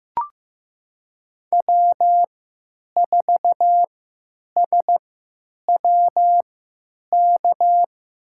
Morse code, heard as a keyed tone, about 15 wpm, spaced as for 8 wpm, 700 Hz. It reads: W4SWK
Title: USA Call Signs - 3x Character Spacing - 15wpm